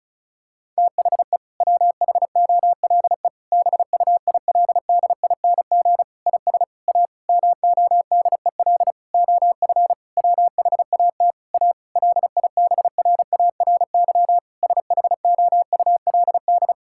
Morse code, heard as a keyed tone, about 34 wpm, 700 Hz. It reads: THE WHOLE BUILDING IS A MODEL OF WHAT A LIBRARY SHOULD
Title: Sentences from Top 1000 Words - Review Part 5 - 35wpm